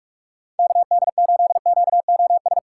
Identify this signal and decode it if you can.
Morse code, heard as a keyed tone, about 46 wpm, 700 Hz. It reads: KD8QOS
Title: USA Call Signs - 6 Characters - 6x Character Spacing - 45wpm